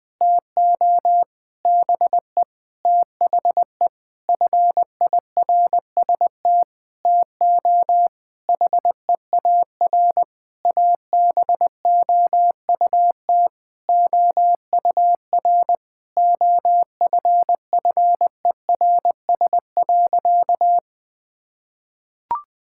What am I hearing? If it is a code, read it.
Morse code, heard as a keyed tone, about 20 wpm, 700 Hz. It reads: TO BE THE FIRST TO HEAR ABOUT OUR OFFERS.
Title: Sentences with 6-Word Phrases - Spoken Beginning - 20wpm